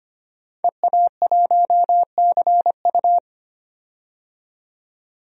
Morse code, heard as a keyed tone, about 25 wpm, 700 Hz. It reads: EA1CU
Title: DXCC-100 Call Signs - 5 Characters - 25wpm